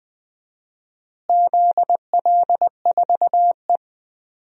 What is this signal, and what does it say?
Morse code, heard as a keyed tone, about 20 wpm, 700 Hz. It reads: ZL4E
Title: DXCC-100 Call Signs - 4 Characters - Repeated 2x - 20wpm